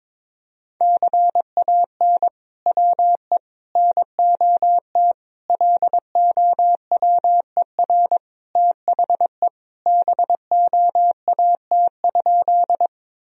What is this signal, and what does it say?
Morse code, heard as a keyed tone, about 22 wpm, 700 Hz. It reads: CAN WE NOT LOWER THE BOAT?
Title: Sentences from Top 1000 Words - Review Part 4 - 22wpm